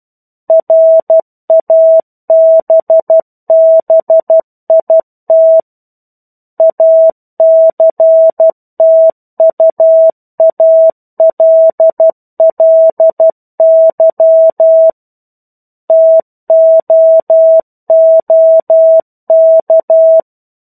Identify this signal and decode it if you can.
Morse code, heard as a keyed tone, about 12 wpm, 650 Hz. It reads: RABBIT ACTUALLY TOOK
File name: 01-Chapter-01 12 WPM